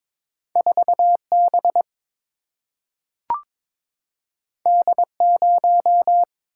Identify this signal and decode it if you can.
Morse code, heard as a keyed tone, about 22 wpm, 700 Hz. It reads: 4 B D 0
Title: Single-Letter Single-Number 22wpm